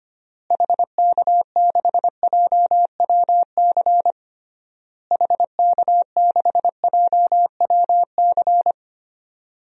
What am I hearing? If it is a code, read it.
Morse code, heard as a keyed tone, about 25 wpm, 700 Hz. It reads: HK6JWC HK6JWC
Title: DXCC-100 Call Signs - 6 Characters - Repeated 3x - 25wpm